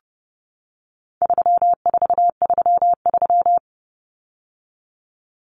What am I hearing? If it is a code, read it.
Morse code, heard as a keyed tone, about 30 wpm, 700 Hz. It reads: 3433
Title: Numbers - 4-Digit 30wpm